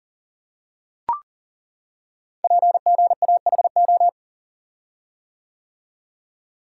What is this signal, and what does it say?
Morse code, heard as a keyed tone, about 40 wpm, 700 Hz. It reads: PGAHO
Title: Code Groups - 5 Letters - 40wpm